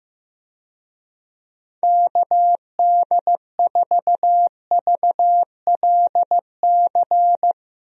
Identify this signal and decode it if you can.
Morse code, heard as a keyed tone, about 15 wpm, 700 Hz. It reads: KD4VLC
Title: USA Call Signs - 6 Characters - Repeated 2x - 15wpm